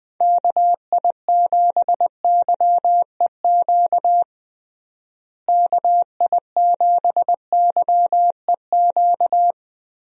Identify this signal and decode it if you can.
Morse code, heard as a keyed tone, about 21 wpm, 700 Hz. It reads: KI7YEQ KI7YEQ
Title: USA Call Signs - 6 Characters - Repeated 3x - 20wpm